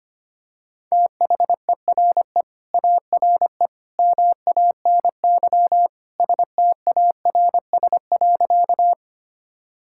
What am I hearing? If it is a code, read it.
Morse code, heard as a keyed tone, about 25 wpm, 700 Hz. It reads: THERE ARE MANY STARS.
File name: practice / 25wpm / Sentences with 2-Word Phrases - 25wpm